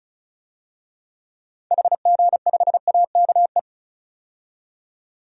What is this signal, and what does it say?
Morse code, heard as a keyed tone, about 36 wpm, 700 Hz. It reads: HG5AKE
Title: DXCC-100 Call Signs - 6 Characters - 35wpm